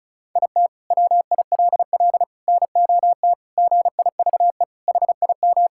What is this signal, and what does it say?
Morse code, heard as a keyed tone, about 36 wpm, 700 Hz. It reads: IT WILL NOT GIVE HIM
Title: Sentences from Top 1600 Words 35wpm